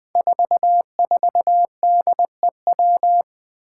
Morse code, heard as a keyed tone, about 20 wpm, 700 Hz. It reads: 44DEW
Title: Five Challenging Letter-Number Combinations 20wpm